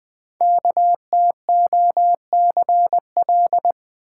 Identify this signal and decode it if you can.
Morse code, heard as a keyed tone, about 20 wpm, 700 Hz. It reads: KTOCL
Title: Code Groups - 5 Letters - 20wpm